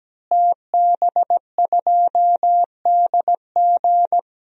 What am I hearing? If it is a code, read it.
Morse code, heard as a keyed tone, about 17 wpm, 700 Hz. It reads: TB2DG
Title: DXCC-100 Call Signs - 5 Characters - Repeated 2x - 17wpm